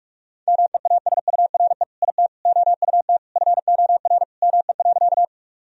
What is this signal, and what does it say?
Morse code, heard as a keyed tone, about 46 wpm, 700 Hz. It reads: MEASURE IT OUT FOR ME.
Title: Sentences from Top 1000 Words - Review Part 2 - 45wpm